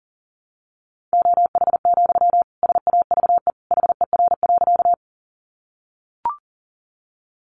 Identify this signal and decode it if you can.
Morse code, heard as a keyed tone, about 40 wpm, 700 Hz. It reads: OH, SAVE HER.
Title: Sentences from Top 1100 Words 40wpm